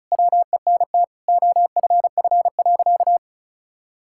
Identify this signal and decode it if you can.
Morse code, heard as a keyed tone, about 36 wpm, 700 Hz. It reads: WENT OFF.
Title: Sentences from Top 1600 Words 35wpm